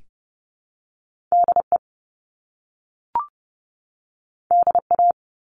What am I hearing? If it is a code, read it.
Morse code, heard as a keyed tone, about 30 wpm, 700 Hz. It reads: DE DA
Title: Top Two Letter Combinations in Words 30wpm